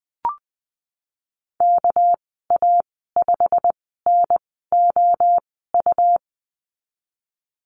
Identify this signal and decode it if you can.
Morse code, heard as a keyed tone, about 20 wpm, spaced as for 15 wpm, 700 Hz. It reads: KA5NOU
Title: 200 Lesson 32g - Letter k - Call Signs 20wpm